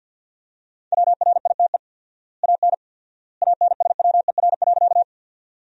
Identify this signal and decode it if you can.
Morse code, heard as a keyed tone, about 52 wpm, 700 Hz. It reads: WRITE AN ANSWER.
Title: 4x Word-Spacing - Sentences from Top 300 Words - 50wpm